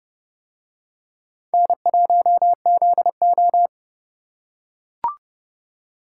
Morse code, heard as a keyed tone, about 30 wpm, 700 Hz. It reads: N1ZO